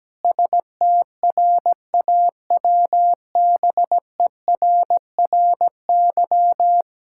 Morse code, heard as a keyed tone, about 17 wpm, 700 Hz. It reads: STRAWBERRY
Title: Pangram Challenge v2 - 17wpm